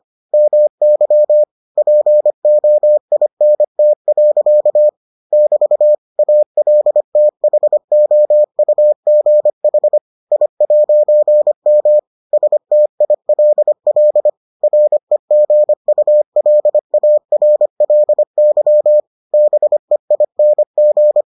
Morse code, heard as a keyed tone, about 25 wpm, 600 Hz. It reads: MY POINT. <BT> ALTHOUGH I'M STILL REGULARLY BEING